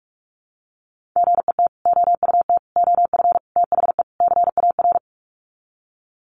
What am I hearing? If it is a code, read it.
Morse code, heard as a keyed tone, about 44 wpm, 700 Hz. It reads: GET OUT OF THE CAR
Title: Sets of 5 Words Encore2 45wpm